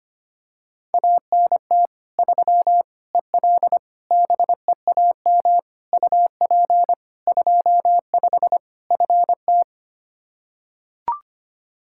Morse code, heard as a keyed tone, about 25 wpm, 700 Hz. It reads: ANT 3 EL BEAM UP 25 FT
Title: Practice QSO - Exchange 2 - Info - 25wpm